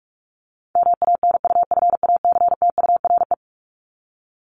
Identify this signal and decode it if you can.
Morse code, heard as a keyed tone, about 44 wpm, 700 Hz. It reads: MANUFACTURE
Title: ING Suffix - Top 100 Words with Base Words 45wpm